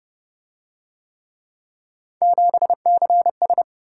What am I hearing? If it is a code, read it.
Morse code, heard as a keyed tone, about 26 wpm, 700 Hz. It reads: 7CS